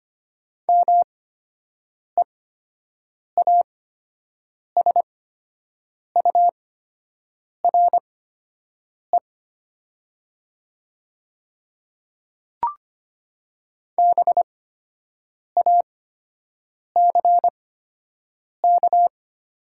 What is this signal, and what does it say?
Morse code, heard as a keyed tone, about 25 wpm, spaced as for 7 wpm, 700 Hz. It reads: MEASURE BACK
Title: 8x Character Spacing - Top 500 Words - 25wpm